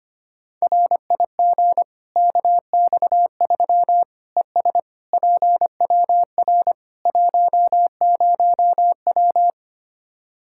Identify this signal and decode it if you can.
Morse code, heard as a keyed tone, about 25 wpm, 700 Hz. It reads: RIG KX3 ES PWR 10W
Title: Practice QSO - Exchange 2 - Info - 25wpm